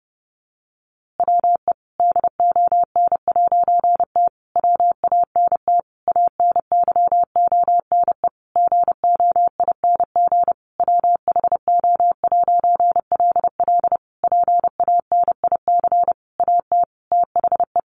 Morse code, heard as a keyed tone, about 30 wpm, 700 Hz. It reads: WE DON'T WANT ANYONE GOING WHO'LL PANIC AT THE
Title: A Sound Of Thunder 30wpm